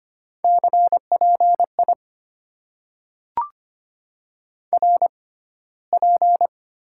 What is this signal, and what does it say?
Morse code, heard as a keyed tone, about 25 wpm, spaced as for 15 wpm, 690 Hz. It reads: CPI RP
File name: Top 90 QSO Elements - 6x Character Spacing - 25wpm